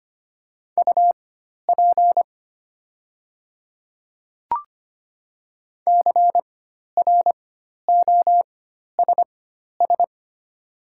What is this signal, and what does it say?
Morse code, heard as a keyed tone, about 25 wpm, spaced as for 12 wpm, 700 Hz. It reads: UP CROSS
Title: 4x Character Spacing - Top 500 Words - 25wpm